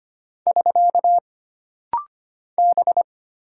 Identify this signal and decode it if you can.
Morse code, heard as a keyed tone, about 25 wpm, 700 Hz. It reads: <SK> B
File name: Single Character (Letter:Number:Punctuation:Prosign) - Rapid-Fire - 25wpm